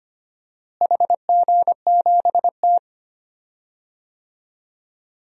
Morse code, heard as a keyed tone, about 25 wpm, 700 Hz. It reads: HG7T
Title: DXCC-100 Call Signs - 4 Characters - Repeated 2x - 25wpm